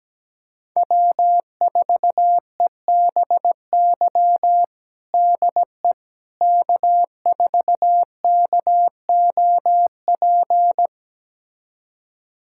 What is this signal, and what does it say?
Morse code, heard as a keyed tone, about 17 wpm, 700 Hz. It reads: W4EBY DE K4KOP
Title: USA Callsign DE Callsign - 5 Characters - 17wpm